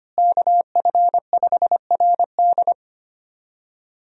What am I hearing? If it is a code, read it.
Morse code, heard as a keyed tone, about 25 wpm, 700 Hz. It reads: KF5RD
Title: USA Call Signs - 5 Characters - Repeated 2x - 25wpm